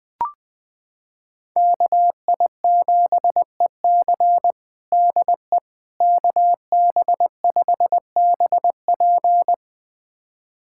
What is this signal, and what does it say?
Morse code, heard as a keyed tone, about 20 wpm, 700 Hz. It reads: KI7EC DE KB5BP
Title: USA Callsign DE Callsign - 5 Characters - 20wpm